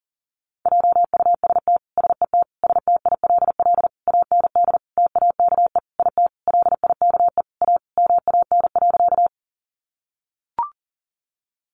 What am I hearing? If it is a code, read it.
Morse code, heard as a keyed tone, about 40 wpm, 700 Hz. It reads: JUST SET STILL AND TAKE IT LIKE A MAN.
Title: Sentences from Top 300 Words 40wpm